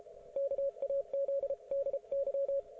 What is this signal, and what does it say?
Morse code, heard as a keyed tone, about 32 wpm, 550 Hz. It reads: KAZDY